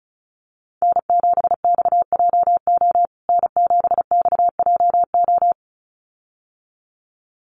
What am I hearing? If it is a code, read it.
Morse code, heard as a keyed tone, about 34 wpm, 700 Hz. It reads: N7XJO N7XJO
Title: USA Call Signs Repeated 2x 35wpm